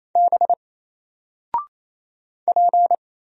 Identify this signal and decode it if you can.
Morse code, heard as a keyed tone, about 27 wpm, 700 Hz. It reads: B P
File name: Single-Letter-Number-28wpm